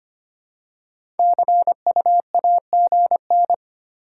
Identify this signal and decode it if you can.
Morse code, heard as a keyed tone, about 25 wpm, 700 Hz. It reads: CU AGN
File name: practice / 25wpm / Top 20 QSO Elements - 25wpm